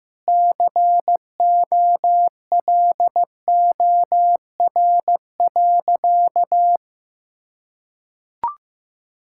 Morse code, heard as a keyed tone, about 15 wpm, 700 Hz. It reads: COLOR.
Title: Sentences from Top 1000 Words - Review Part 6 - 15wpm